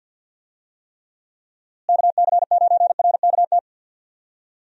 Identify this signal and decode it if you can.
Morse code, heard as a keyed tone, about 51 wpm, 700 Hz. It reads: KC9RKT
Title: USA Call Signs - 6 Characters - Repeated 2x - 50wpm